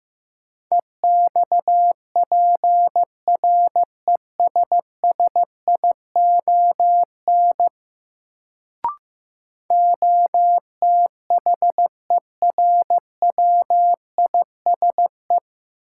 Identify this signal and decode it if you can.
Morse code, heard as a keyed tone, about 15 wpm, 700 Hz. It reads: EXPRESSION OTHERWISE
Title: Sentences from Top 1200 Words - New Word Drill - 15wpm